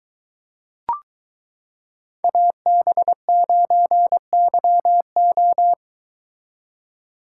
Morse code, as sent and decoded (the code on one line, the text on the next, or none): .- -... ----. -.-- ---
AB9YO